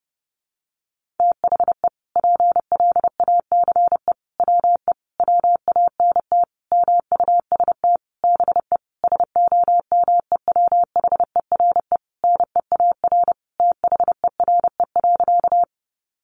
Morse code, heard as a keyed tone, about 30 wpm, 700 Hz. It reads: THE PLACE WE WANT MUST BE SOMEWHERE NEAR THERE.
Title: Sentences from Top 1400 Words 30wpm